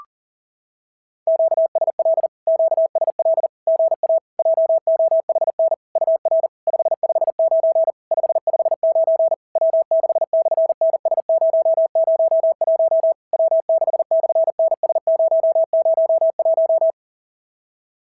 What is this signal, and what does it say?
Morse code, heard as a keyed tone, about 40 wpm, 660 Hz. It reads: QSL QSL GA JOHN UR 559 559 W6/NS001 W6/NS001
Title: SOTA QSO Practice 40wpm